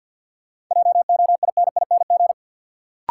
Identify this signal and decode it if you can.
Morse code, heard as a keyed tone, about 47 wpm, 700 Hz. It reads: JOINING
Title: Sets of 5 Words - New Word Drill - 50wpm